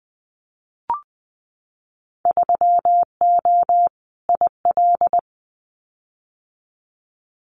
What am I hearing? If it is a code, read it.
Morse code, heard as a keyed tone, about 20 wpm, 700 Hz. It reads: 3O IL
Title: Winter Field Day Exchanges 20wpm